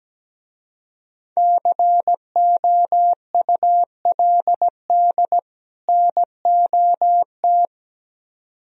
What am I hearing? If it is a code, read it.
Morse code, heard as a keyed tone, about 17 wpm, 700 Hz. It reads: COULD NOT
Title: Speed-Racing - Sets of 2 Words 17wpm